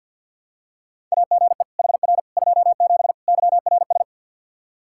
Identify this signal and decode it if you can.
Morse code, heard as a keyed tone, about 52 wpm, 700 Hz. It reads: AGE HR 27 YRS
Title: Practice QSO - Exchange 3 - More Info - 50wpm